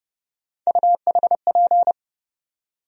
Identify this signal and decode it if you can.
Morse code, heard as a keyed tone, about 30 wpm, 700 Hz. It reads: UHP
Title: USA Call Sign Postfixes - 3 Letters 30wpm